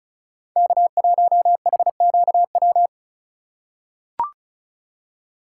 Morse code, heard as a keyed tone, about 36 wpm, 700 Hz. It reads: K1HQW
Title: USA Call Signs - 5 Characters - 3x Character Spacing - 35wpm